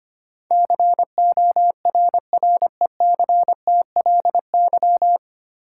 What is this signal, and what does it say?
Morse code, heard as a keyed tone, about 25 wpm, 700 Hz. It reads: CORRECTLY
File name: practice / 25wpm / 9 Letter Words - First 3 Letters Given - 25wpm